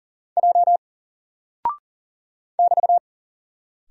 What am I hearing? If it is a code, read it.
Morse code, heard as a keyed tone, about 40 wpm, 700 Hz. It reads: J <BT>
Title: Single Character (Letter:Number:Punctuation:Prosign) Alt 40wpm